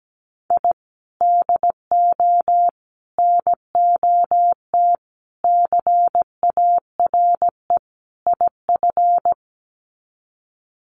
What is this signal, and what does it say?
Morse code, heard as a keyed tone, about 17 wpm, 700 Hz. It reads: I DO NOT CARE IF
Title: Sets of 5 Words 17wpm